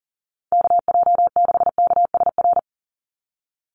Morse code, heard as a keyed tone, about 40 wpm, 700 Hz. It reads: KJ6KSR
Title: USA Call Signs 40wpm